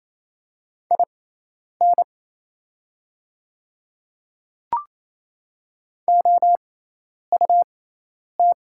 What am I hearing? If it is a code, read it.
Morse code, heard as a keyed tone, about 27 wpm, spaced as for 10 wpm, 700 Hz. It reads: IN OUT
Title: 6x Character Spacing - Top 100 Words - 28wpm